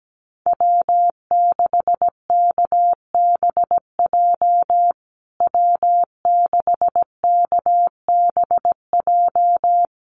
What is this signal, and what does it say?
Morse code, heard as a keyed tone, about 17 wpm, 700 Hz. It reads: W6KBJ W6KBJ
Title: USA Call Signs Repeated 3x 17wpm